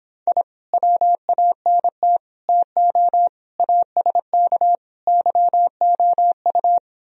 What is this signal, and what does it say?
Morse code, heard as a keyed tone, about 26 wpm, 700 Hz. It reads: I WANT TO ASK YOU
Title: Speed-Racing - Sets of 5 Words 22wpm